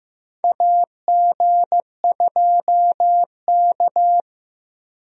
Morse code, heard as a keyed tone, about 15 wpm, 700 Hz. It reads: AG2K
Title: USA Call Signs - 4 Characters - Repeated 3x - 15wpm